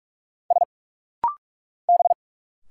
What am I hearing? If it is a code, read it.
Morse code, heard as a keyed tone, about 46 wpm, 690 Hz. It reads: S B
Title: Single Letters - Rapid-Fire 45wpm